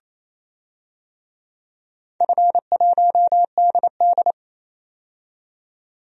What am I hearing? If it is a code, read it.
Morse code, heard as a keyed tone, about 28 wpm, 700 Hz. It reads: F1DD
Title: DXCC-100 Call Signs - 4 Characters - 28wpm